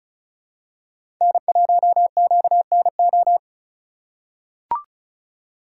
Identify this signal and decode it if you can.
Morse code, heard as a keyed tone, about 34 wpm, 700 Hz. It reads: N1QNO
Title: USA Call Signs - 5 Characters - 3x Character Spacing - 35wpm